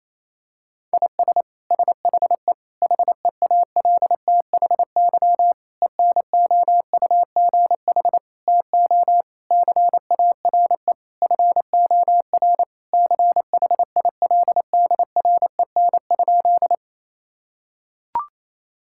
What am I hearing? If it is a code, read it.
Morse code, heard as a keyed tone, about 28 wpm, 700 Hz. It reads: IS SHE HEALTHY ENOUGH TO CARE FOR CHILDREN?
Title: Sentences from Top 1000 Words - Review Part 5 - 28wpm